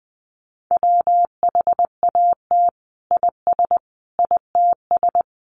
Morse code, heard as a keyed tone, about 20 wpm, 700 Hz. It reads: WHAT IS ITS
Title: Sentences from Top 600 Words 20wpm